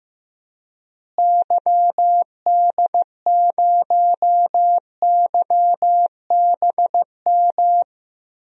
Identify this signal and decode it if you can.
Morse code, heard as a keyed tone, about 15 wpm, 700 Hz. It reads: YD0YBM
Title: DXCC-100 Call Signs - 6 Characters - Repeated 3x - 15wpm